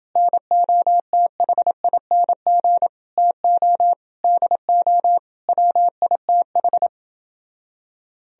Morse code, heard as a keyed tone, about 30 wpm, 700 Hz. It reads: NOTHING TO DO WITH